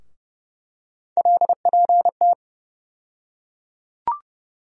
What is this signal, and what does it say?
Morse code, heard as a keyed tone, about 30 wpm, 700 Hz. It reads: LPT
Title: USA Call Sign Postfixes - 3 Letters 30wpm